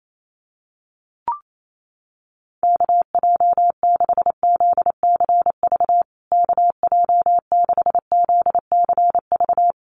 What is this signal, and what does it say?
Morse code, heard as a keyed tone, about 28 wpm, 700 Hz. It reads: KJ6ZCV KJ6ZCV
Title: USA Call Signs Repeated 3x 28wpm